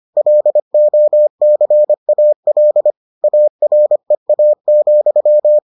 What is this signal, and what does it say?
Morse code, heard as a keyed tone, about 25 wpm, 600 Hz. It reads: LOCAL AREA,